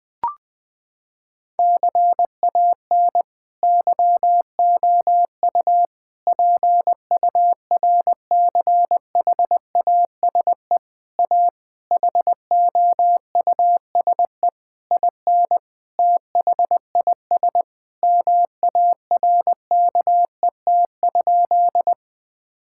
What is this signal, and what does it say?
Morse code, heard as a keyed tone, about 20 wpm, 700 Hz. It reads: CAN YOU PURCHASE A HOUSE IN THIS MARKET?
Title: Sentences from Top 1000 Words - Review Part 5 - 20wpm